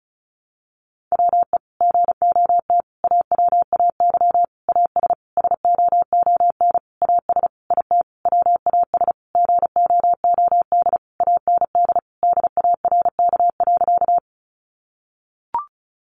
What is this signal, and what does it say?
Morse code, heard as a keyed tone, about 34 wpm, 700 Hz. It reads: WE GOT AWAY AS SOON AS IT WAS GOOD AND DARK.
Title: Sentences from Top 600 Words 35wpm